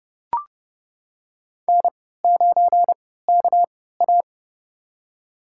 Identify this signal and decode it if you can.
Morse code, heard as a keyed tone, about 30 wpm, spaced as for 17 wpm, 700 Hz. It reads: N9KA
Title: USA Call Signs - 4 Characters - 3x Character Spacing - 30wpm